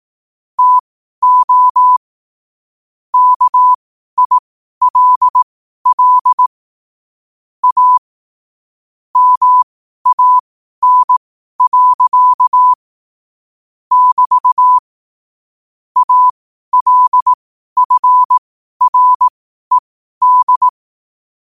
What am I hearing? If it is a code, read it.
Morse code, heard as a keyed tone, about 18 wpm, spaced as for 12 wpm, 1000 Hz. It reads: TO KILL A MAN. <BT> ALFRED
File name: qotdfiles / 12wpm